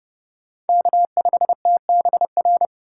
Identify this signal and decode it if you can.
Morse code, heard as a keyed tone, about 30 wpm, 700 Hz. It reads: K5TBR
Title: USA Call Signs - 6x Character Spacing - 30wpm